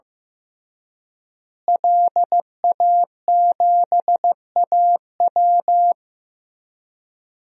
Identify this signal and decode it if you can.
Morse code, heard as a keyed tone, about 15 wpm, 700 Hz. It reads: LA7AW
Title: DXCC-100 Call Signs - 5 Characters - Repeated 3x - 15wpm